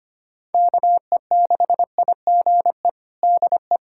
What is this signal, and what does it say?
Morse code, heard as a keyed tone, about 25 wpm, 700 Hz. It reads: KE6IGE DE
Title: USA Callsign DE Callsign - 6 Characters - 25wpm